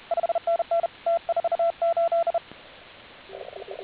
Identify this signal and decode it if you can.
Morse code, heard as a keyed tone, about 39 wpm, 680 Hz. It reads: 5NN T48